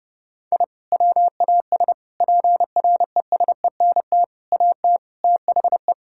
Morse code, heard as a keyed tone, about 30 wpm, 700 Hz. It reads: I WAS PRESENT AT THE
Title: Sentences from Top 1000 Words - Review Part 5 - 30wpm